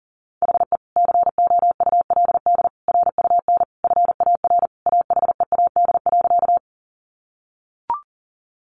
Morse code, heard as a keyed tone, about 40 wpm, 700 Hz. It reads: HE COULD RUN FAR AHEAD.
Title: Sentences from Top 1100 Words 40wpm